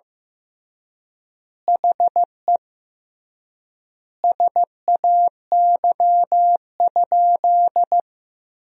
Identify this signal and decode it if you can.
Morse code, heard as a keyed tone, about 15 wpm, 700 Hz. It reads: HE SAY?
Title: 3x Word-Spacing - Sentences from Top 200 Words - 15wpm